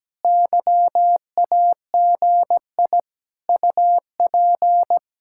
Morse code, heard as a keyed tone, about 17 wpm, 700 Hz. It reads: YAGI UP
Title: Speed-Racing Common QSO Phrases - 17wpm